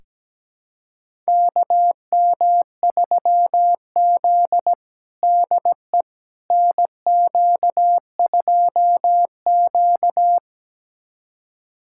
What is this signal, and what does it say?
Morse code, heard as a keyed tone, about 17 wpm, 700 Hz. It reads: KM3Z DE NQ2Q
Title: USA Callsign DE Callsign - 4 Characters - 17wpm